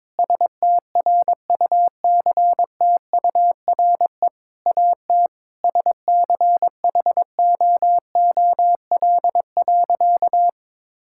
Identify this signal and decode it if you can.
Morse code, heard as a keyed tone, about 22 wpm, 700 Hz. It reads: STRUCTURE AT SCHOOL.
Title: Sentences from Top 1000 Words - Review Part 6 - 22wpm